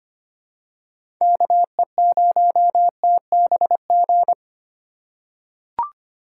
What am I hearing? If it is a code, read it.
Morse code, heard as a keyed tone, about 25 wpm, 700 Hz. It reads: KE0TBG
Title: USA Call Signs - 6 Characters - 8x Character Spacing - 25wpm